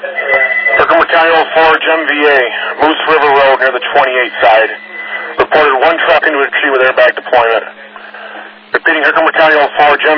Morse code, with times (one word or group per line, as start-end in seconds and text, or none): none